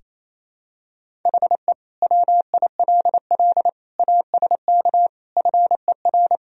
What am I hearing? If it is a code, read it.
Morse code, heard as a keyed tone, about 28 wpm, 700 Hz. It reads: HE WILL ASK FER
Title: Sentences from Top 200 Words and Abbreviations 28wpm